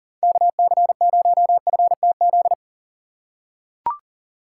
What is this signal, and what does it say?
Morse code, heard as a keyed tone, about 38 wpm, 700 Hz. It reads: KC0FTZ
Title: USA Call Signs - 6x Character Spacing - 40wpm